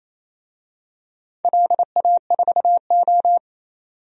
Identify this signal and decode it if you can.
Morse code, heard as a keyed tone, about 28 wpm, 700 Hz. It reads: LA4O
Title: DXCC-100 Call Signs - 4 Characters - 28wpm